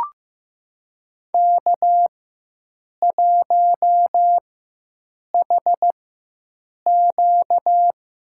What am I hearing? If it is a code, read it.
Morse code, heard as a keyed tone, about 15 wpm, spaced as for 7 wpm, 700 Hz. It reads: K1HQ
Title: USA Call Signs - 4x Character Spacing - 15wpm